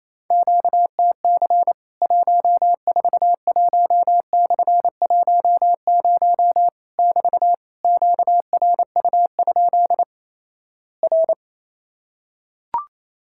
Morse code, heard as a keyed tone, about 28 wpm, 700 Hz. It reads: QTC 141/10 <BT> QRU? R
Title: Worked All Europe DX Contest - QTC Exchange - 28wpm